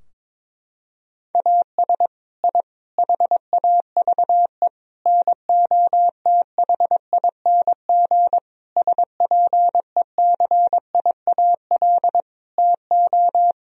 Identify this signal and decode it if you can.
Morse code, heard as a keyed tone, about 22 wpm, 700 Hz. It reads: AS I HAVE NOTHING SPECIAL TO
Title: Sentences from Top 1000 Words - Review Part 3 - 22wpm